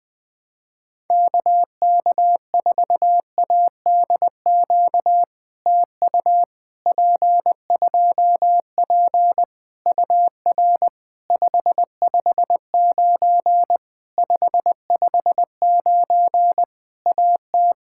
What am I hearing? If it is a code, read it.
Morse code, heard as a keyed tone, about 20 wpm, 700 Hz. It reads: KK4ADQ TU P2P UR 559 559 AT
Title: POTA QSO Practice - 20wpm